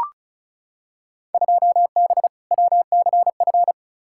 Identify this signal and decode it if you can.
Morse code, heard as a keyed tone, about 36 wpm, 690 Hz. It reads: 2B WCF
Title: Field Day Exchanges Encore - 35wpm